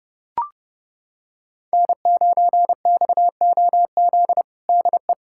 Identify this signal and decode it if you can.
Morse code, heard as a keyed tone, about 32 wpm, 700 Hz. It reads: N9XOZ DE
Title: USA Callsign DE Callsign - 5 Characters - 30wpm